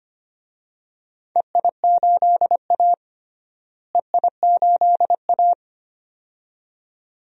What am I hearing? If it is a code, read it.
Morse code, heard as a keyed tone, about 25 wpm, 700 Hz. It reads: EI8A EI8A
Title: DXCC-100 Call Signs - 4 Characters - Repeated 2x - 25wpm